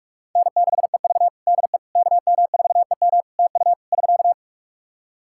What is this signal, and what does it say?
Morse code, heard as a keyed tone, about 46 wpm, 700 Hz. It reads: N6EV DE KM4EM TU <SK>
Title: Practice QSO - Exchange 4 - Ending - 45wpm